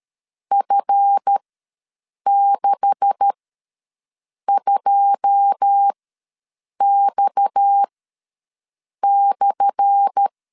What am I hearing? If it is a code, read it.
Morse code, heard as a keyed tone, about 13 wpm, spaced as for 6 wpm, 800 Hz. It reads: F62X/
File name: LEZIONI / 10 SIGNS / 10A SLASH ws 12wpm 3443